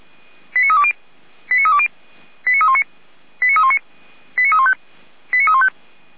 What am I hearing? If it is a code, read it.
Five-tone selective call (ZVEI), heard as quick runs of five tones, tones of about 70 ms. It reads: 89320 89320 89319 89319 89315 89315